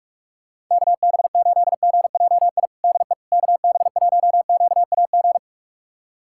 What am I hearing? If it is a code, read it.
Morse code, heard as a keyed tone, about 46 wpm, 700 Hz. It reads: KB8GJI DE KB1QAG
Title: USA Callsign DE Callsign 45wpm